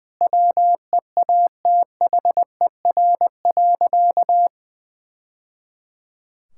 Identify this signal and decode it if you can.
Morse code, heard as a keyed tone, about 20 wpm, 700 Hz. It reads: WEATHER.
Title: Sentences from Top 1000 Words - Review Part 5 - 20wpm